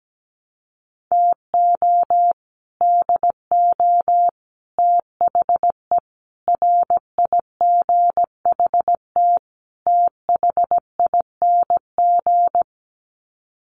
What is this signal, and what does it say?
Morse code, heard as a keyed tone, about 17 wpm, 700 Hz. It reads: TO DO THE RIGHT THING
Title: Sets of 5 Words 17wpm